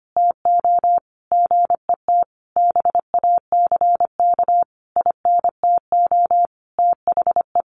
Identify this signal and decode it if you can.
Morse code, heard as a keyed tone, about 25 wpm, 700 Hz. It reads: TO GET BACK INTO THE
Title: Sentences from Top 1400 Words 25wpm